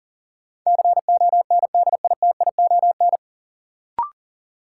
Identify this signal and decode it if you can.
Morse code, heard as a keyed tone, about 40 wpm, 700 Hz. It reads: CONDITION
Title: Sentences from Top 1000 Words - New Word Drill - 40wpm